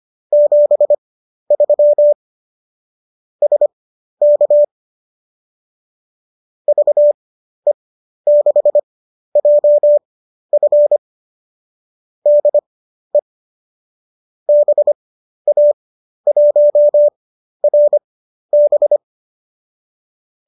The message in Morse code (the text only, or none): 73 SK VE6JF DE BA1RB